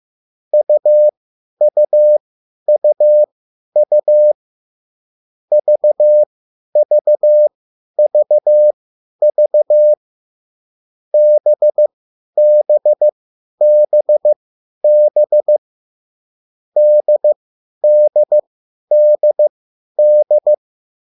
Morse code, heard as a keyed tone, about 15 wpm, spaced as for 10 wpm, 600 Hz. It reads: UUUU VVVV BBBB DDDD
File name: QSL / CW lesson